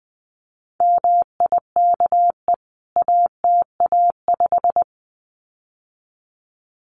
Morse code, heard as a keyed tone, about 20 wpm, 700 Hz. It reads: MIKE ATA5
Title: CWT Exchange 20wpm